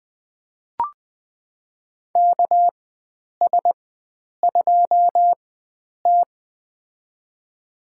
Morse code, heard as a keyed tone, about 20 wpm, spaced as for 9 wpm, 700 Hz. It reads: KS2T
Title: USA Call Signs - 4 Characters - 4x Character Spacing - 20wpm